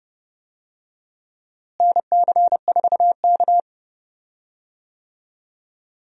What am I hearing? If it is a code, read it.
Morse code, heard as a keyed tone, about 30 wpm, 700 Hz. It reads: NC4K
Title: USA Call Signs - 4 Characters - Repeated 2x - 30wpm